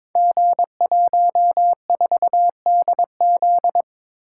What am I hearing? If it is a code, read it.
Morse code, heard as a keyed tone, about 22 wpm, 700 Hz. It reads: G14DZ
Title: Code Groups - 5 Letter:Numbers - 22wpm